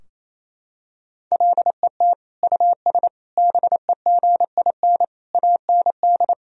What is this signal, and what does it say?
Morse code, heard as a keyed tone, about 28 wpm, 700 Hz. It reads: LET US BEGIN AND
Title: Sentences from Top 1000 Words - Review Part 3 - 28wpm